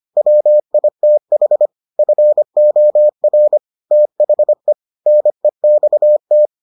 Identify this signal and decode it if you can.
Morse code, heard as a keyed tone, about 25 wpm, 600 Hz. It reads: WITH FOR THE NEXT